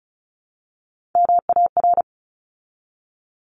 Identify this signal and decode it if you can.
Morse code, heard as a keyed tone, about 34 wpm, 700 Hz. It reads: MAR
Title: ARRL Sections 35wpm